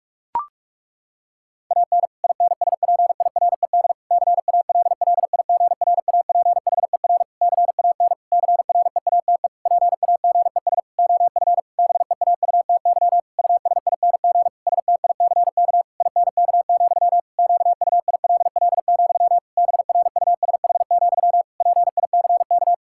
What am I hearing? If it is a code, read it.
Morse code, heard as a keyed tone, about 42 wpm, 700 Hz. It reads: AN INSPIRED CALLIGRAPHER CAN CREATE PAGES OF BEAUTY USING STICK INK, QUILL, BRUSH, PICK
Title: Pangram Challenge v2 - 45wpm